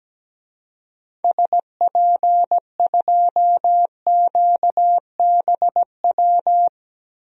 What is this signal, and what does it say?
Morse code, heard as a keyed tone, about 17 wpm, 700 Hz. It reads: SP2QBW
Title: DXCC-100 Call Signs - 6 Characters - 17wpm